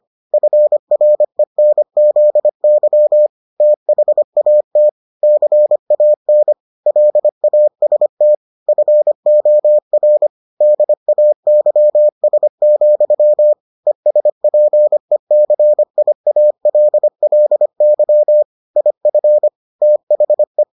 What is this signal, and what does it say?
Morse code, heard as a keyed tone, about 25 wpm, 600 Hz. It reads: FRENZY THAT CAN LAST FOR DAYS, ESPECIALLY IF THE